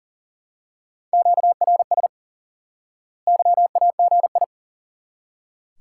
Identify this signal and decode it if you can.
Morse code, heard as a keyed tone, about 40 wpm, 700 Hz. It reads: QRS YAGI
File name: IQE Course 3 - Lesson 89 - YAGI - Two Elements - 40wpm